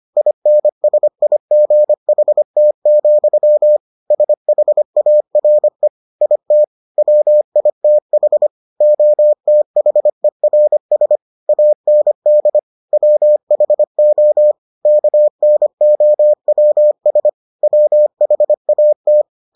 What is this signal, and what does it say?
Morse code, heard as a keyed tone, about 25 wpm, 600 Hz. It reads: INSIGHT, SHARE IT WITH OTHERS AND WHO KNOWS WHAT